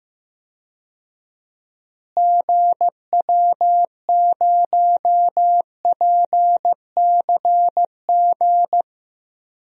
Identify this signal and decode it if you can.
Morse code, heard as a keyed tone, about 15 wpm, 700 Hz. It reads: GW0PCG
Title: DXCC-100 Call Signs - 6 Characters - Repeated 3x - 15wpm